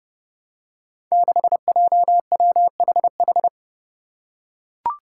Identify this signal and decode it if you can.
Morse code, heard as a keyed tone, about 30 wpm, 700 Hz. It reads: 6JWHH
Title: Five Challenging Letter-Number Combinations 30wpm